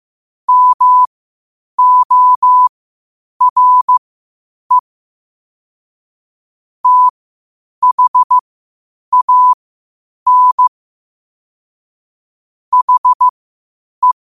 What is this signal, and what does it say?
Morse code, heard as a keyed tone, about 15 wpm, spaced as for 8 wpm, 1000 Hz. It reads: MORE THAN HE
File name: qotdfiles / 8wpm